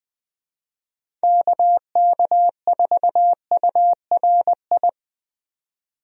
Morse code, heard as a keyed tone, about 20 wpm, 700 Hz. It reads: KK4URI
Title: USA Call Signs - 6 Characters - 20wpm